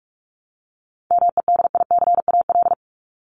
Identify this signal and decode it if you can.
Morse code, heard as a keyed tone, about 44 wpm, 700 Hz. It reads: MEDICAL
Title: Top 751-1000 Words 45wpm